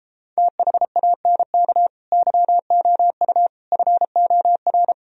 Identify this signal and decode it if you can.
Morse code, heard as a keyed tone, about 33 wpm, 700 Hz. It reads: THANK YOU FOR